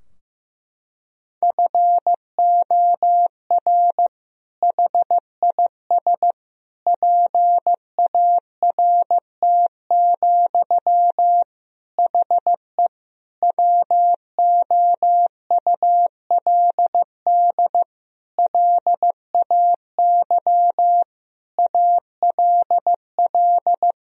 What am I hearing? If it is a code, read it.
Morse code, heard as a keyed tone, about 14 wpm, 700 Hz. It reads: FOR HIS PART, HE WOULD LAY ALL
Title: Sentences from Top 1000 Words - Review Part 2 - 15wpm